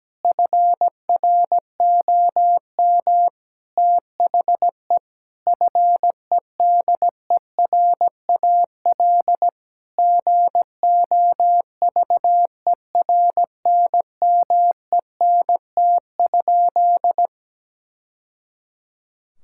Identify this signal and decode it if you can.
Morse code, heard as a keyed tone, about 17 wpm, 700 Hz. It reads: FROM THE FEDERAL GOVERNMENT?
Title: Sentences from Top 1000 Words - Review Part 5 - 17wpm